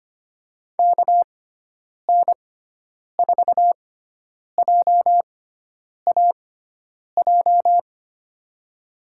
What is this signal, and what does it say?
Morse code, heard as a keyed tone, about 25 wpm, spaced as for 9 wpm, 700 Hz. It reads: KN4JAJ
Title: USA Call Signs - 6 Characters - 6x Character Spacing - 25wpm